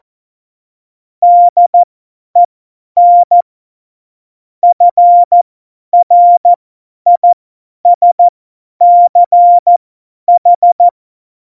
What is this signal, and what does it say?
Morse code, heard as a keyed tone, about 14 wpm, spaced as for 10 wpm, 700 Hz. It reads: DEN FRISCH
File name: Morsekurs / Witze / Witz 17. November 2020, 50 F70